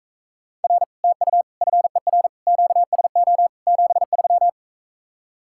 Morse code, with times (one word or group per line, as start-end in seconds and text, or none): 0.6-0.8 R
1.0-1.4 TU
1.6-2.3 FER
2.5-3.5 QSO
3.7-4.5 73